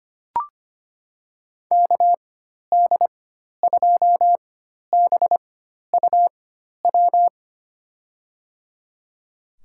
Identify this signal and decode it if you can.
Morse code, heard as a keyed tone, about 25 wpm, spaced as for 12 wpm, 700 Hz. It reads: KD2BUW
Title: USA Call Signs - 6 Characters - 4x Character Spacing - 25wpm